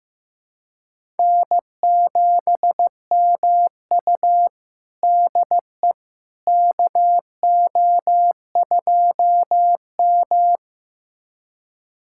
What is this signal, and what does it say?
Morse code, heard as a keyed tone, about 16 wpm, 700 Hz. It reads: N7MU DE KO2M
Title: USA Callsign DE Callsign - 4 Characters - 15wpm